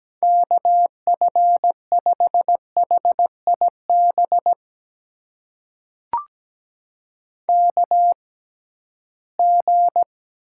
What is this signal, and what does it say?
Morse code, heard as a keyed tone, about 17 wpm, spaced as for 13 wpm, 700 Hz. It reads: KF5HIB KG
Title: USA Call Signs - 6 Characters - 6x Character Spacing - 17wpm